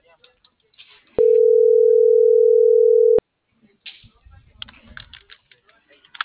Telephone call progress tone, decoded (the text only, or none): ringback tone